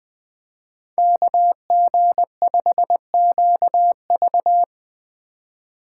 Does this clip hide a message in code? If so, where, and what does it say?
Yes: Morse code from 1.0 to 4.6 s, heard as a keyed tone, about 21 wpm, 700 Hz. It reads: KG5QV